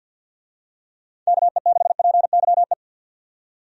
Morse code, heard as a keyed tone, about 52 wpm, 700 Hz. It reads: KE6PYE
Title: USA Call Signs - 6 Characters - 50wpm